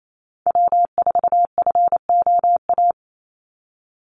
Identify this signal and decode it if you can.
Morse code, heard as a keyed tone, about 28 wpm, 700 Hz. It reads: W4FOA
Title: Call Signs - 28wpm